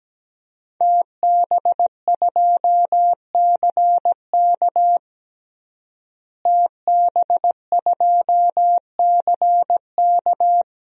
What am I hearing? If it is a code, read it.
Morse code, heard as a keyed tone, about 17 wpm, 700 Hz. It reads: TB2CK TB2CK
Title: DXCC-100 Call Signs - 5 Characters - Repeated 2x - 17wpm